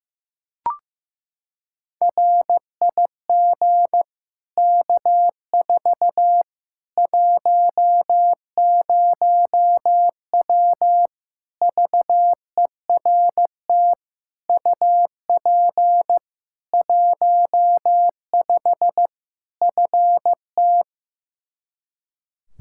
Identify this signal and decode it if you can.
Morse code, heard as a keyed tone, about 16 wpm, 700 Hz. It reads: RIG K4 10W VERT UP 15 FT
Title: Practice QSO - Exchange 2 - Info - 15wpm